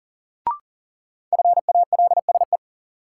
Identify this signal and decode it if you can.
Morse code, heard as a keyed tone, about 39 wpm, 700 Hz. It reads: FALSE
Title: Sentences from Top 1600 Words - New Word Drill - 40wpm